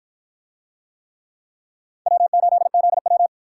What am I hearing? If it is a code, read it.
Morse code, heard as a keyed tone, about 53 wpm, 700 Hz. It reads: W8ZW